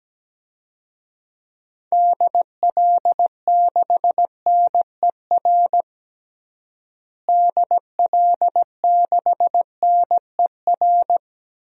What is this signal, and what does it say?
Morse code, heard as a keyed tone, about 17 wpm, 700 Hz. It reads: DL6NER DL6NER
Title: DXCC-100 Call Signs - 6 Characters - Repeated 3x - 17wpm